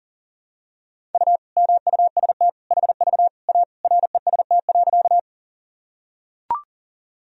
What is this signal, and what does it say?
Morse code, heard as a keyed tone, about 40 wpm, 700 Hz. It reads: U MUST HV A REST.
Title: Sentences from Top 500 Words and Abbreviations 40wpm